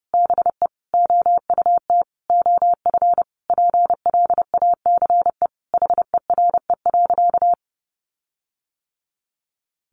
Morse code, heard as a keyed tone, about 30 wpm, 700 Hz. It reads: BE OUT OF PLACE HERE.
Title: Sentences from Top 1500 Words 30wpm